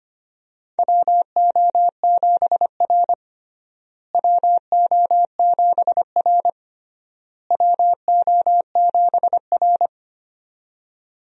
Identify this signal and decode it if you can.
Morse code, heard as a keyed tone, about 25 wpm, 700 Hz. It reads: WO7R WO7R WO7R
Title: USA Call Signs - 4 Characters - Repeated 3x - 25wpm